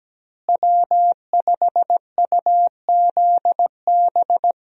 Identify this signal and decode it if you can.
Morse code, heard as a keyed tone, about 17 wpm, 700 Hz. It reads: W5UZB
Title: USA Call Signs - 5 Characters - Repeated 3x - 17wpm